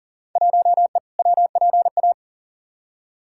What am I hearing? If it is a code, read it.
Morse code, heard as a keyed tone, about 40 wpm, 700 Hz. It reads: 1E WPA
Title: Field Day Exchanges - 40wpm